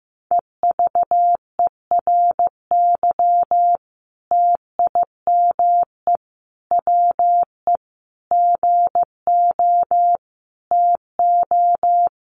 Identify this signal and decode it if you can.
Morse code, heard as a keyed tone, about 15 wpm, 700 Hz. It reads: EVERY TIME WE GO TO
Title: Sentences from Top 1300 Words 15wpm